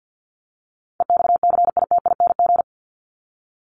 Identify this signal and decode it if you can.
Morse code, heard as a keyed tone, about 52 wpm, 700 Hz. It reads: EXCITING
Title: Top 1751-2000 Words 50wpm